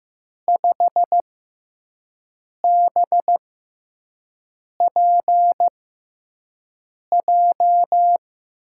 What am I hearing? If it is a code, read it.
Morse code, heard as a keyed tone, about 15 wpm, spaced as for 5 wpm, 700 Hz. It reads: 5BPJ
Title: USA Call Signs - 5 Characters - 6x Character Spacing - 15wpm